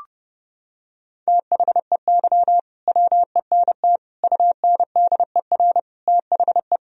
Morse code, heard as a keyed tone, about 30 wpm, 700 Hz. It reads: THEY WENT UNDER THE
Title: Sentences from Top 1000 Words - Review Part 1 - 30wpm